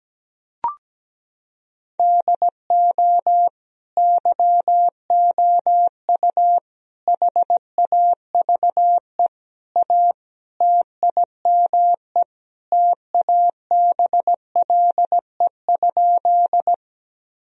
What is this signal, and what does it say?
Morse code, heard as a keyed tone, about 17 wpm, 700 Hz. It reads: DO YOU HAVE A TIME TABLE?
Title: Sentences from Top 1000 Words - Review Part 2 - 17wpm